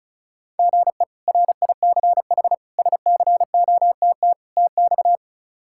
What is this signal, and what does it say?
Morse code, heard as a keyed tone, about 34 wpm, 700 Hz. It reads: GE RICH SCOTT TX
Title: K1USN SST QSO Practice - 35wpm